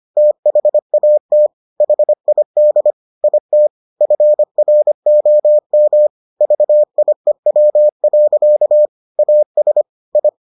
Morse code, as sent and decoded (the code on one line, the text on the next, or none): - .... .- - / .... .. -.. / .. - / ..-. .-. --- -- / ...- .. .-- .-.-.- / .- ... / ..
THAT HID IT FROM VIEW. AS I